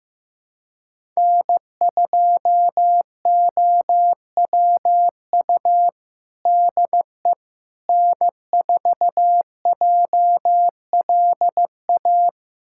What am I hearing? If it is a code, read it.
Morse code, heard as a keyed tone, about 15 wpm, 700 Hz. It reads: N2OWU DE N4JLA
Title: USA Callsign DE Callsign - 5 Characters - 15wpm